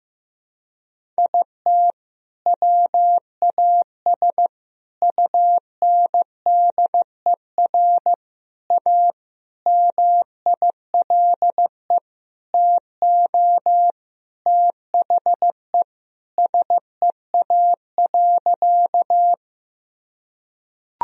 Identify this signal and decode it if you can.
Morse code, heard as a keyed tone, about 15 wpm, 700 Hz. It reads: IT WAS UNDER A MILE TO THE SEA.
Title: Sentences from Top 1000 Words - Review Part 1 - 15wpm